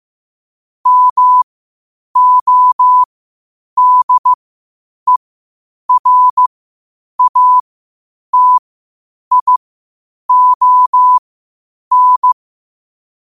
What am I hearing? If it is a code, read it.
Morse code, heard as a keyed tone, about 15 wpm, spaced as for 8 wpm, 1000 Hz. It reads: MODERATION
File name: qotdfiles / 8wpm